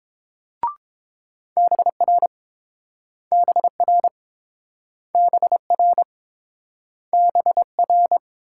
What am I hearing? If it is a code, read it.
Morse code, heard as a keyed tone, about 30 wpm, 700 Hz. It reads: BR BR BR BR